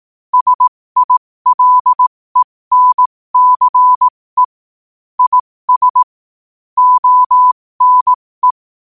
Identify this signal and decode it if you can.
Morse code, heard as a keyed tone, about 18 wpm, spaced as for 15 wpm, 1000 Hz. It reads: SILENCE IS ONE
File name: qotdfiles / 15wpm